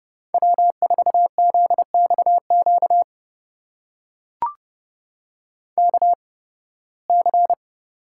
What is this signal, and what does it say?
Morse code, heard as a keyed tone, about 30 wpm, spaced as for 20 wpm, 700 Hz. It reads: W4ZXQ KC
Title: USA Call Signs - 5 Characters - 8x Character Spacing - 30wpm